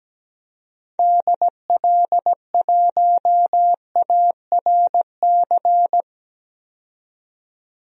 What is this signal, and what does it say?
Morse code, heard as a keyed tone, about 17 wpm, 700 Hz. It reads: DL1ARC
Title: DXCC-100 Call Signs - 6 Characters - 17wpm